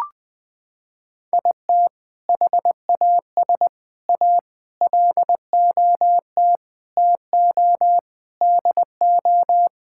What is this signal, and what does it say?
Morse code, heard as a keyed tone, about 19 wpm, 700 Hz. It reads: IT HAS A LOT TO DO
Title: Sentences with 5-Word Phrases - 20wpm